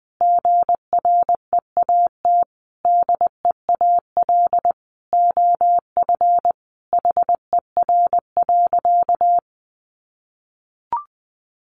Morse code, heard as a keyed tone, about 20 wpm, 700 Hz. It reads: GREAT DEAL OF HER.
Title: Sentences from Top 800 Words 20wpm